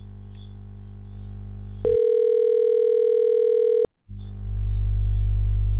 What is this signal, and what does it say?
Ringback tone (North America), 440 + 480 Hz.